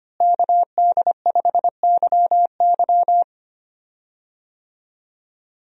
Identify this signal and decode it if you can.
Morse code, heard as a keyed tone, about 25 wpm, 700 Hz. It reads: KD5YY
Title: USA Call Signs - 5 Characters - Repeated 2x - 25wpm